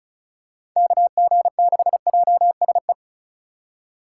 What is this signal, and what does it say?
Morse code, heard as a keyed tone, about 36 wpm, 700 Hz. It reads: KG6JSE
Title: USA Call Signs - 4x Character Spacing - 35wpm